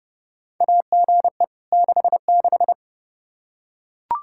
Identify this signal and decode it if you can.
Morse code, heard as a keyed tone, about 30 wpm, 700 Hz. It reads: AGE 66
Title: K1USN SST QSO Practice 30wpm